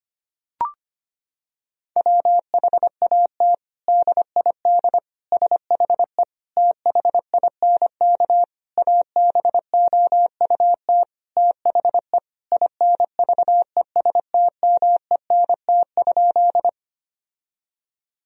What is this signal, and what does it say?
Morse code, heard as a keyed tone, about 25 wpm, 700 Hz. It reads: WHAT DID SHE THINK ABOUT THE INVESTMENT?
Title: Sentences from Top 1000 Words - Review Part 4 - 25wpm